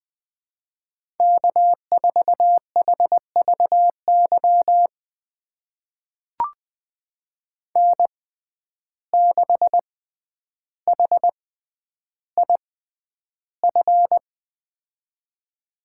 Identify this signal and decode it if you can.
Morse code, heard as a keyed tone, about 20 wpm, spaced as for 10 wpm, 700 Hz. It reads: K4HVY N6HIF
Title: USA Call Signs - 5 Characters - 6x Character Spacing - 20wpm